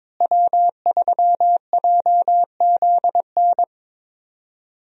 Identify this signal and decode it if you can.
Morse code, heard as a keyed tone, about 22 wpm, 700 Hz. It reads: W3JZN
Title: USA Call Signs - 5 Characters - Repeated 2x - 22wpm